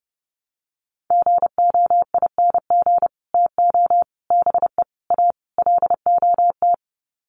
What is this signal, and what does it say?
Morse code, heard as a keyed tone, about 30 wpm, 700 Hz. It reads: GOING TO BE A LOT
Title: Sets of 5 Words Encore 30wpm